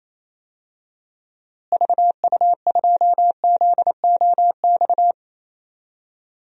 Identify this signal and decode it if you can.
Morse code, heard as a keyed tone, about 28 wpm, 700 Hz. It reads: VU2ZOX
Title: DXCC-100 Call Signs - 6 Characters - Repeated 2x - 28wpm